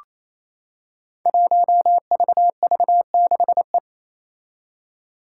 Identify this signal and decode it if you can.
Morse code, heard as a keyed tone, about 27 wpm, 700 Hz. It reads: 1VV6E
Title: Five Challenging Letter-Number Combinations 28wpm